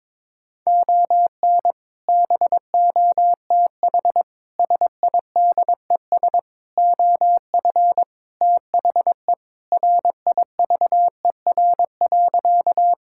Morse code, heard as a keyed tone, about 22 wpm, 700 Hz. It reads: ON BOTH SIDES OF THE RIVER.
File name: practice / 22wpm / Sentences with 4-Word Phrases - 22wpm